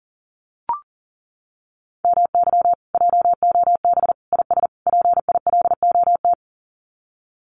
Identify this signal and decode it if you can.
Morse code, heard as a keyed tone, about 40 wpm, 700 Hz. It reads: MY JOB IS PILOT
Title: Common QSO Phrases Encore 40wpm